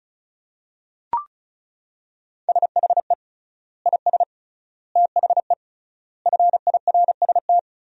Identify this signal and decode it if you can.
Morse code, heard as a keyed tone, about 34 wpm, 700 Hz. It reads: SHE IS THE FIRST
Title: 3x Word-Spacing - Sentences from Top 400 Words - 35wpm